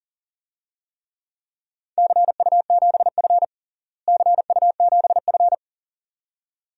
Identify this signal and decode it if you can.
Morse code, heard as a keyed tone, about 37 wpm, 700 Hz. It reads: CU7F CU7F
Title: DXCC-100 Call Signs - 4 Characters - Repeated 2x - 40wpm